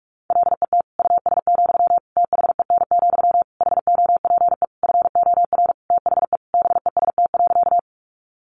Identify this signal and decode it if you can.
Morse code, heard as a keyed tone, about 44 wpm, 700 Hz. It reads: LET US, THEN, HOPE FOR THE BEST.
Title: Sentences from Top 1000 Words 45wpm